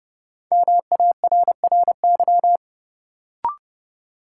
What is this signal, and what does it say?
Morse code, heard as a keyed tone, about 32 wpm, 700 Hz. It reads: MARRY
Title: Sentences from Top 1200 Words - New Word Drill - 30wpm